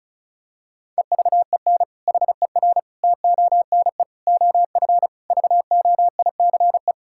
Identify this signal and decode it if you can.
Morse code, heard as a keyed tone, about 36 wpm, 700 Hz. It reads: EVEN HER TONE OF VOICE
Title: Sentences from Top 1600 Words 35wpm